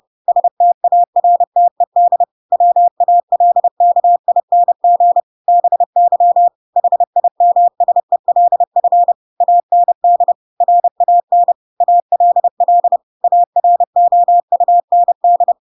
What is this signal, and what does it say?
Morse code, heard as a keyed tone, about 30 wpm, 700 Hz. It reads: STARTED WALKING BY HIMSELF AND RAN ALL AROUND
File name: Pinocchio / Pinocchio ch3 30 30 ik1hge